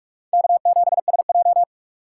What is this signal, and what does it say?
Morse code, heard as a keyed tone, about 44 wpm, 700 Hz. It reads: K7SJ
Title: USA Call Signs - 4 Characters - 2x Character Spacing - 45wpm